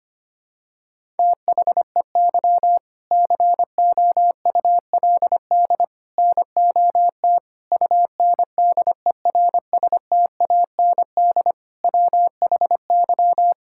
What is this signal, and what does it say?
Morse code, heard as a keyed tone, about 25 wpm, 700 Hz. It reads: THEY COULD NOT UNDERSTAND WHY
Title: Sentences from Top 1000 Words - Review Part 6 - 25wpm